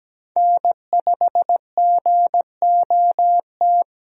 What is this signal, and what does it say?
Morse code, heard as a keyed tone, about 17 wpm, 700 Hz. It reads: N5GOT
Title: USA Call Signs - 5 Characters - Repeated 2x - 17wpm